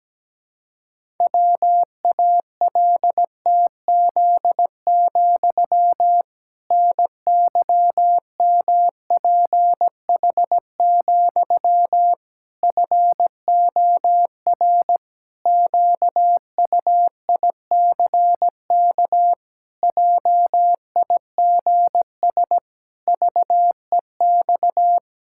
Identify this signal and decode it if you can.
Morse code, heard as a keyed tone, about 17 wpm, 700 Hz. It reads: WALTZ, NYMPH, FOR QUICK JIGS VEX
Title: Pangram Challenge v2 - 17wpm